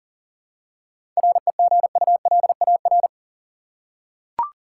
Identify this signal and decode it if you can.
Morse code, heard as a keyed tone, about 40 wpm, 700 Hz. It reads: REGULAR
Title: Seven Letter Words - 40wpm